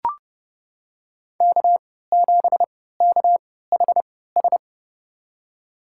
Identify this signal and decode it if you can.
Morse code, heard as a keyed tone, about 30 wpm, spaced as for 17 wpm, 700 Hz. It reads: K7KHS
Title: USA Call Signs - 5 Characters - 3x Character Spacing - 30wpm